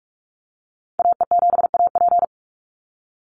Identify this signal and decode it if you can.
Morse code, heard as a keyed tone, about 44 wpm, 700 Hz. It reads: AE7AP
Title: NA SOTA Chaser Callsigns 45wpm